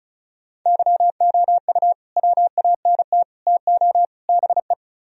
Morse code, heard as a keyed tone, about 39 wpm, 700 Hz. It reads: YOU WANT TO BE